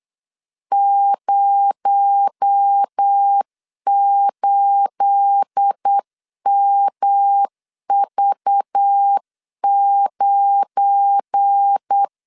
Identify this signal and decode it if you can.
Morse code, heard as a keyed tone, about 8 wpm, 800 Hz. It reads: 08MV9
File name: LEZIONI / 9 28 8wpm 3512